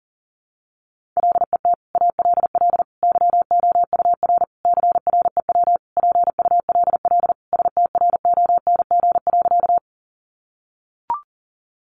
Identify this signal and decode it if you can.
Morse code, heard as a keyed tone, about 40 wpm, 700 Hz. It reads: LET ALL YOUR CREW PULL STRONG.
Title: Sentences from Top 1500 Words 40wpm